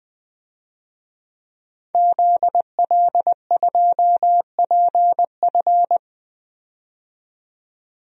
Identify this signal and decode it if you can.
Morse code, heard as a keyed tone, about 20 wpm, 700 Hz. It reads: ZL2PF
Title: DXCC-100 Call Signs - 5 Characters - 20wpm